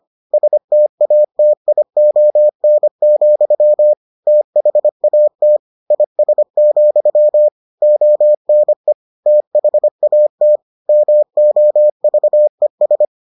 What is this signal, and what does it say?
Morse code, heard as a keyed tone, about 25 wpm, 600 Hz. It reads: STATION, THAT IS, ONE THAT MOVES